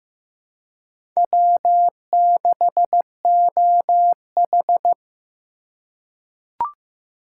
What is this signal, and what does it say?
Morse code, heard as a keyed tone, about 15 wpm, 700 Hz. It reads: W6OH
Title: USA Call Signs - 4 Characters - 2x Character Spacing - 15wpm